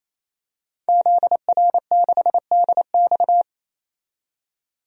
Morse code, heard as a keyed tone, about 28 wpm, 700 Hz. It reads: ZR6DX
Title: DXCC-100 Call Signs - 5 Characters - 28wpm